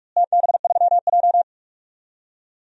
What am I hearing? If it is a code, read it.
Morse code, heard as a keyed tone, about 44 wpm, 700 Hz. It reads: TB3J
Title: DXCC-100 Call Signs - 4 Characters - Repeated 2x - 45wpm